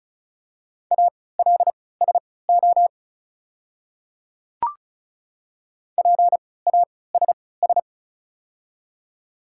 Morse code, heard as a keyed tone, about 34 wpm, spaced as for 20 wpm, 700 Hz. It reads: ALSO PASS
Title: 3x Character Spacing - Top 500 Words - 35wpm